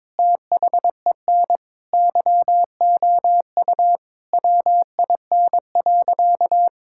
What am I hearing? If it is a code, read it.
Morse code, heard as a keyed tone, about 22 wpm, 700 Hz. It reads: THEN YOU WIN.
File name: practice / 22wpm / Sentences with 8-Word Phrases - 22wpm